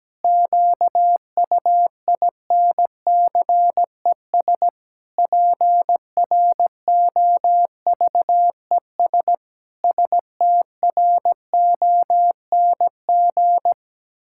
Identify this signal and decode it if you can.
Morse code, heard as a keyed tone, about 17 wpm, 700 Hz. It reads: QUINCES PROVES STRONG
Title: Pangram Challenge v2 - 17wpm